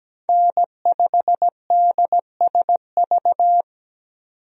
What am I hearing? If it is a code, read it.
Morse code, heard as a keyed tone, about 17 wpm, 700 Hz. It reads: N5DSV
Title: USA Call Signs - 5 Characters - 17wpm